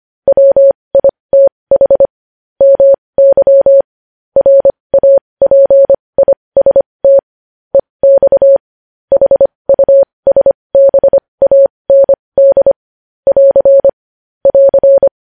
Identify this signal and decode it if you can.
Morse code, heard as a keyed tone, about 25 wpm, spaced as for 20 wpm, 550 Hz. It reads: WITH MY RAPIST EX HUSBAND <AR> <AR>
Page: News Headlines in Morse Code at 20 WPM